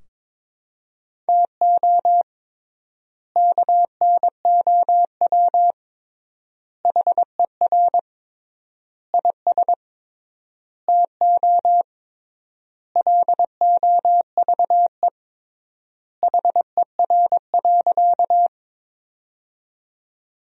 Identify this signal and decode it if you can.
Morse code, heard as a keyed tone, about 22 wpm, 700 Hz. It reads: TO KNOW HER IS TO LOVE HER.
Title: 3x Word-Spacing - Sentences from Top 400 Words - 22wpm